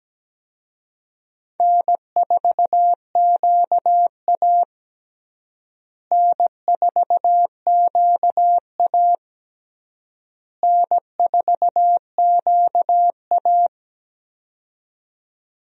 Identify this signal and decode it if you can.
Morse code, heard as a keyed tone, about 17 wpm, 700 Hz. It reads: N4QA N4QA N4QA
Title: USA Call Signs - 4 Characters - Repeated 3x - 17wpm